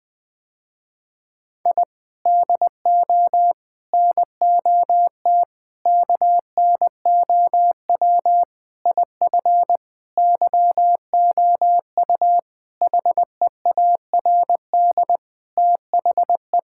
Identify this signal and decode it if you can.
Morse code, heard as a keyed tone, about 21 wpm, 700 Hz. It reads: I DO NOT KNOW IF YOU HEARD THE
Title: Sentences with 5-Word Phrases - Spoken Beginning - 20wpm